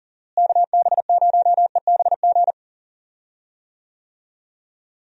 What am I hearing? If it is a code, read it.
Morse code, heard as a keyed tone, about 39 wpm, 700 Hz. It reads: KB0EBG
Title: USA Call Signs - 6 Characters - Repeated 2x - 40wpm